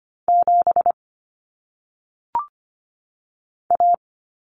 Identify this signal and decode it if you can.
Morse code, heard as a keyed tone, about 25 wpm, 700 Hz. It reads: 7 A
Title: Common Cut Numbers - 1-Digit 25wpm